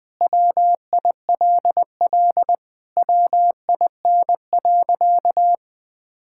Morse code, heard as a keyed tone, about 21 wpm, 700 Hz. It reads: WILL WIN.
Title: Sentences with 6-Word Phrases - Spoken Beginning - 20wpm